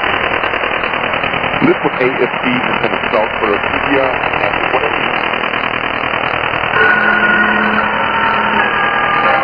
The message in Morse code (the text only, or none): none